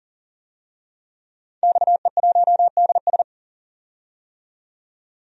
Morse code, heard as a keyed tone, about 40 wpm, 700 Hz. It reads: XE1DS
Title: DXCC-100 Call Signs - 5 Characters - 40wpm